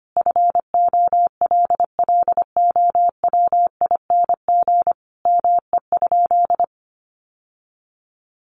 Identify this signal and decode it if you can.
Morse code, heard as a keyed tone, about 25 wpm, 700 Hz. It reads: FOLLOWING ME?
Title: Sentences from Top 700 Words 25wpm